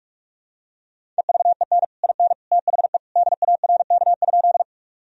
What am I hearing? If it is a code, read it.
Morse code, heard as a keyed tone, about 47 wpm, 700 Hz. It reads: EVEN IN THE DARK?
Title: Sentences from Top 1000 Words - Review Part 1 - 45wpm